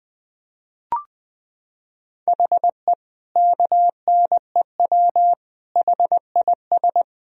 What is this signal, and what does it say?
Morse code, heard as a keyed tone, about 21 wpm, 700 Hz. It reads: HE KNEW HIS
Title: Sentences from Top 1000 Words - Review Part 1 - 20wpm